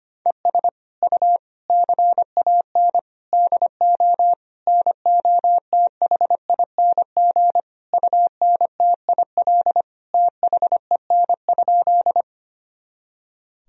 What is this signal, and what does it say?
Morse code, heard as a keyed tone, about 25 wpm, 700 Hz. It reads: ES U CAN DO NOTHING UNTIL THEN?
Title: Sentences from Top 500 Words and Abbreviations 25wpm